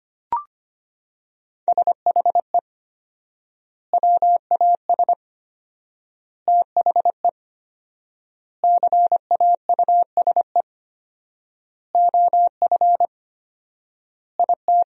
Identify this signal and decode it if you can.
Morse code, heard as a keyed tone, about 25 wpm, 700 Hz. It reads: SHE WAS THE CAUSE OF IT
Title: 4x Word-Spacing - Sentences from Top 200 Words - 25wpm